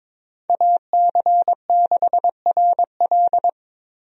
Morse code, heard as a keyed tone, about 22 wpm, 700 Hz. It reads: AC6RL
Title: USA Call Signs - 5 Characters - 22wpm